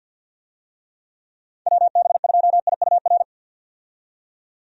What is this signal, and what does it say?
Morse code, heard as a keyed tone, about 49 wpm, 700 Hz. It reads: WB2IUR
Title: USA Call Signs - 6 Characters - 50wpm